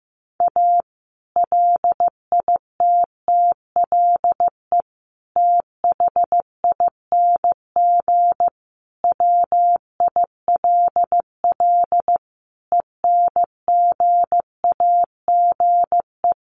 Morse code, heard as a keyed tone, about 15 wpm, 700 Hz. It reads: A LITTLE THING WILL ENGAGE
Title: Sentences from Top 1500 Words 15wpm